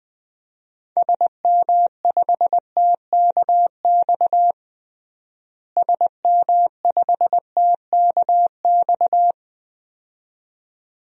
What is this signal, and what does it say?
Morse code, heard as a keyed tone, about 21 wpm, 700 Hz. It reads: SM5TKX SM5TKX
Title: DXCC-100 Call Signs - 6 Characters - Repeated 2x - 20wpm